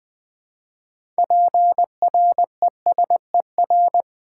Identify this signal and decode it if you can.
Morse code, heard as a keyed tone, about 20 wpm, 700 Hz. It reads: PRESER
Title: Build A Word - 6 Letters At A Time - 12-Letter Words - 20wpm